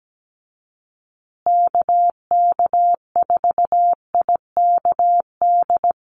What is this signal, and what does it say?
Morse code, heard as a keyed tone, about 17 wpm, 700 Hz. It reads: KK4IKD
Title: USA Call Signs Repeated 2x 17wpm